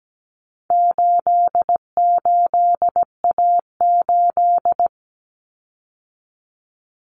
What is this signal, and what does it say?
Morse code, heard as a keyed tone, about 17 wpm, 700 Hz. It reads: 88A8
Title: Common Cut Numbers - 4-Digit 17wpm